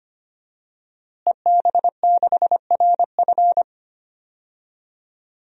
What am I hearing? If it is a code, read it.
Morse code, heard as a keyed tone, about 25 wpm, 700 Hz. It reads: EB6RF
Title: DXCC-100 Call Signs - 5 Characters - Repeated 3x - 25wpm